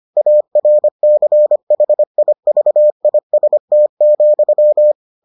Morse code, heard as a keyed tone, about 25 wpm, 600 Hz. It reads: ARCHIVIST,